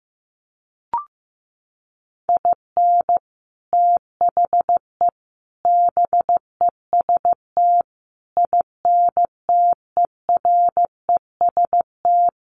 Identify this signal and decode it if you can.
Morse code, heard as a keyed tone, about 15 wpm, 700 Hz. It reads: IN THE BEST INTEREST
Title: Sets of 5 Words Encore 15wpm